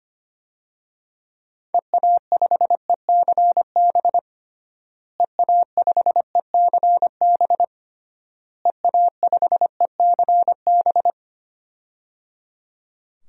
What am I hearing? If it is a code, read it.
Morse code, heard as a keyed tone, about 25 wpm, 700 Hz. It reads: EA5ECB EA5ECB EA5ECB
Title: DXCC-100 Call Signs - 6 Characters - Repeated 3x - 25wpm